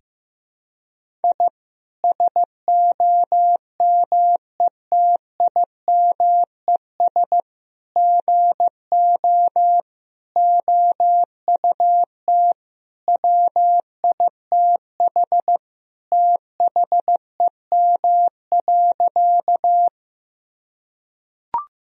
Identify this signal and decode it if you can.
Morse code, heard as a keyed tone, about 15 wpm, 700 Hz. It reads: I SOMETIMES GO OUT WITH THEM.
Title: Sentences from Top 1000 Words - Review Part 3 - 15wpm